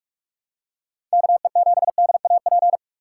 Morse code, heard as a keyed tone, about 45 wpm, 700 Hz. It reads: KE7DAP
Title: USA Call Signs - 6 Characters - 4x Character Spacing - 45wpm